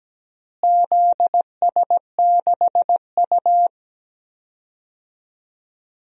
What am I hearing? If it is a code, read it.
Morse code, heard as a keyed tone, about 17 wpm, 700 Hz. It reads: ZS6U
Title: DXCC-100 Call Signs - 4 Characters - Repeated 3x - 17wpm